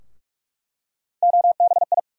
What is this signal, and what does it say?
Morse code, heard as a keyed tone, about 44 wpm, 700 Hz. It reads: OBI
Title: USA Call Sign Postfixes - 3 Letters 45wpm